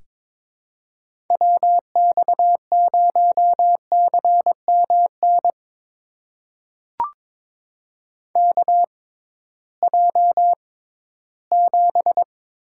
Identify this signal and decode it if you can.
Morse code, heard as a keyed tone, about 22 wpm, spaced as for 14 wpm, 700 Hz. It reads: WX0CMN KJ7
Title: USA Call Signs - 6 Characters - 6x Character Spacing - 22wpm